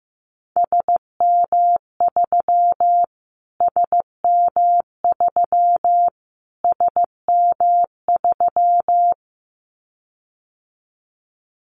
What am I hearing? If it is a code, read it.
Morse code, heard as a keyed tone, about 15 wpm, 700 Hz. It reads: SM3 SM3 SM3
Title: Callsign Prefixes for Top 25 DXCC - Repeated 3x - 15wpm